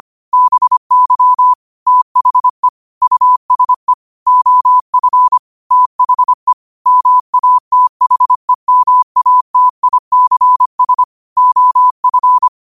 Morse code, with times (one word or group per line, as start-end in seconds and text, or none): 0.3-1.5 BY
1.9-2.7 THE
3.0-3.9 USE
4.3-5.4 OF
5.7-6.5 THE
6.9-11.0 MATHEMATICS
11.4-12.5 OF